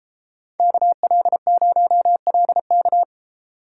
Morse code, heard as a keyed tone, about 30 wpm, 700 Hz. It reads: KL0LK